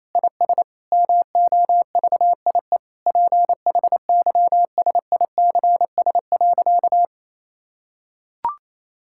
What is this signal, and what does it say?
Morse code, heard as a keyed tone, about 28 wpm, 700 Hz. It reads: IS MOVIE PHYSICS.
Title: Sentences from Top 1000 Words - Review Part 5 - 28wpm